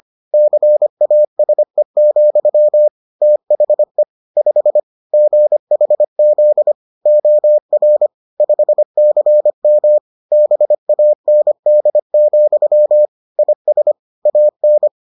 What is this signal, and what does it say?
Morse code, heard as a keyed tone, about 25 wpm, 600 Hz. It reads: CASE, THE 5 GHZ OR 5CM BAND, IS AN